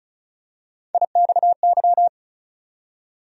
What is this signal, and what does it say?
Morse code, heard as a keyed tone, about 36 wpm, 700 Hz. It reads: IXY
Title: USA Call Sign Postfixes - 3 Letters 35wpm